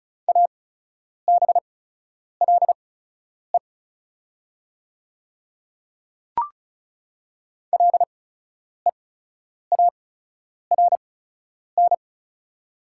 Morse code, heard as a keyed tone, about 36 wpm, spaced as for 10 wpm, 700 Hz. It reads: ABLE LEARN
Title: 8x Character Spacing - Top 500 Words - 35wpm